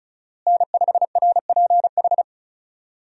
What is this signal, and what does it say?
Morse code, heard as a keyed tone, about 36 wpm, 700 Hz. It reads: N5RPH
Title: USA Call Signs - 5 Characters - Repeated 2x - 35wpm